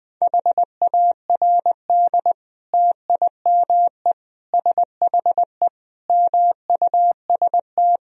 Morse code, heard as a keyed tone, about 20 wpm, 700 Hz. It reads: HARD TIME SHE MUST
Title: Sentences from Top 1000 Words - Review Part 1 - 20wpm